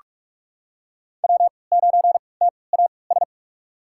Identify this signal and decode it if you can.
Morse code, heard as a keyed tone, about 46 wpm, spaced as for 26 wpm, 700 Hz. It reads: W9TAS
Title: USA Call Signs - 5 Characters - 3x Character Spacing - 45wpm